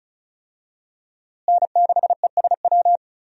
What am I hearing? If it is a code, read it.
Morse code, heard as a keyed tone, about 34 wpm, 700 Hz. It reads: N6ESW
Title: USA Call Signs - 5 Characters - Repeated 2x - 35wpm